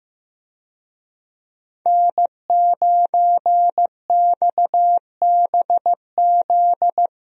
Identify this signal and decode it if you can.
Morse code, heard as a keyed tone, about 15 wpm, 700 Hz. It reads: N9XBZ
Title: USA Call Signs - 5 Characters - 15wpm